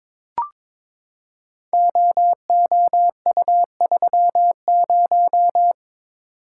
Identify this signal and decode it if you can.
Morse code, heard as a keyed tone, about 23 wpm, 700 Hz. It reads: OOU30
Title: Code Groups - 5 Letter:Numbers - 22wpm